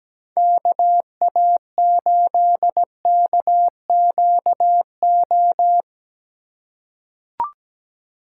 Morse code, heard as a keyed tone, about 17 wpm, 700 Hz. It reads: KA8KQO
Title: USA Call Signs - 2x Character Spacing - 17wpm